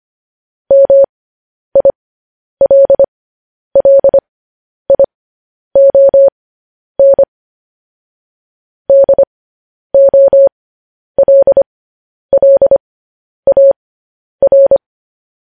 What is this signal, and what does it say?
Morse code, heard as a keyed tone, about 25 wpm, spaced as for 10 wpm, 550 Hz. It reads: MILLION DOLLAR
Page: news-headlines-in-morse-code-at-10-wpm